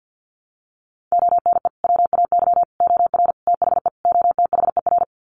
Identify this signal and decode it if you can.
Morse code, heard as a keyed tone, about 50 wpm, 700 Hz. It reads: ONE WAY OR THE OTHER
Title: Sets of 5 Words 50wpm